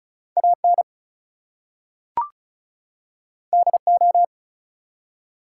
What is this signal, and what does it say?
Morse code, heard as a keyed tone, about 34 wpm, 700 Hz. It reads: AN DO
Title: Two Letter Words - 35wpm